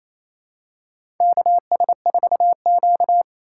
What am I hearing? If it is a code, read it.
Morse code, heard as a keyed tone, about 28 wpm, 700 Hz. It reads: KS4Q
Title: USA Call Signs - 4x Character Spacing - 28wpm